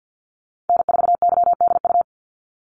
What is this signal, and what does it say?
Morse code, heard as a keyed tone, about 50 wpm, 700 Hz. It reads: N4YDU
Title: Call Signs Repeated 3x - 50wpm